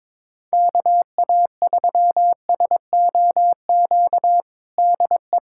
Morse code, heard as a keyed tone, about 22 wpm, 700 Hz. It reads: KA3SOQ DE
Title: USA Callsign DE Callsign 22wpm